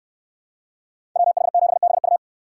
Morse code, heard as a keyed tone, about 64 wpm, 700 Hz. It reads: WH7BU